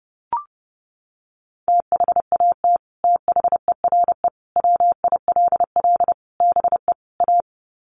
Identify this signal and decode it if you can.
Morse code, heard as a keyed tone, about 30 wpm, 700 Hz. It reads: THAT THERE WILL BE A
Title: Sets of 5 Words Encore2 30wpm